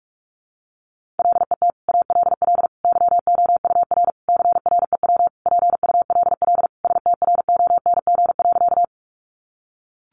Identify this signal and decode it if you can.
Morse code, heard as a keyed tone, about 44 wpm, 700 Hz. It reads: LET ALL YOUR CREW PULL STRONG.
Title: Sentences from Top 1500 Words 45wpm